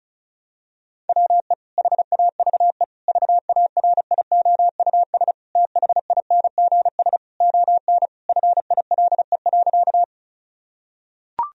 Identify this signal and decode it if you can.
Morse code, heard as a keyed tone, about 36 wpm, 700 Hz. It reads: WE HAVE VARIOUS THINGS ON FILE.
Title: Sentences from Top 1000 Words - Review Part 4 - 35wpm